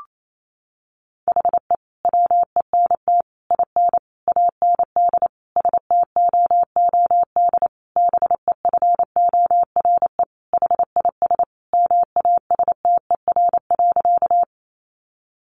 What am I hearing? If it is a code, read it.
Morse code, heard as a keyed tone, about 28 wpm, 700 Hz. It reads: HE WENT IN AND STOOD BEFORE HIS MASTER.
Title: Sentences from Top 1300 Words 28wpm